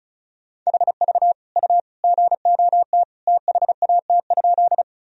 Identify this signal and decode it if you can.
Morse code, heard as a keyed tone, about 36 wpm, 700 Hz. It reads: HV U GOT THAT?
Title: Sentences from Top 300 Words and Abbreviations 35wpm